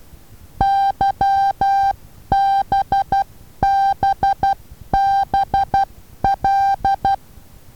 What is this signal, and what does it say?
Morse code, heard as a keyed tone, about 12 wpm, 790 Hz. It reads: YBBBL
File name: Taviro HA6VA B